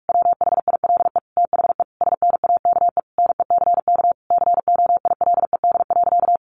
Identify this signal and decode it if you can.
Morse code, heard as a keyed tone, about 44 wpm, 700 Hz. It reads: WHILE THE SNAKE NECK COILED.
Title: A Sound Of Thunder 45wpm